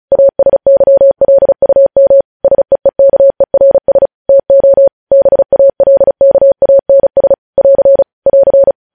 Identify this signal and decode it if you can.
Morse code, heard as a keyed tone, about 36 wpm, 550 Hz. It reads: ASYLUM SEEKERS TO BALKANS <AR> <AR>
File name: podcast / News headlines-1x-35x35